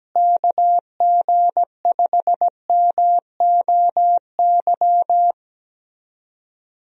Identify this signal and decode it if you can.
Morse code, heard as a keyed tone, about 17 wpm, 700 Hz. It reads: KG5MOY
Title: USA Call Signs - 6 Characters - Repeated 2x - 17wpm